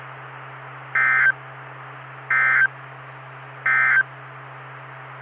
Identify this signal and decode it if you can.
Emergency Alert System (SAME) tones, heard as harsh buzzing bursts. It is heard from 0.8 to 4.1 s.
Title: Saline, AK SKYWARN Repeater System - W5RHS